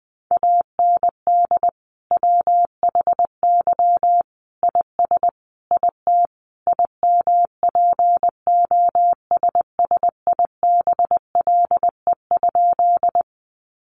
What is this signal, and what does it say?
Morse code, heard as a keyed tone, about 20 wpm, 700 Hz. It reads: AND WHY IS IT IMPOSSIBLE?
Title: Sentences from Top 1100 Words 20wpm